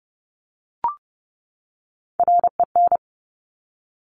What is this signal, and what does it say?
Morse code, heard as a keyed tone, about 30 wpm, 700 Hz. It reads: REN
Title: Top Three Letter Combinations in Words 30wpm